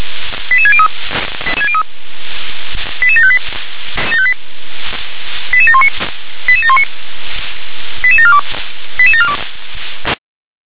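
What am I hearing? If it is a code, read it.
Five-tone selective call (ZVEI), heard as quick runs of five tones, tones of about 70 ms. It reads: 88603 88758 88719 88719 88742 88742
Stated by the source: ZVEI five-tone selective call 88714 received by SDR.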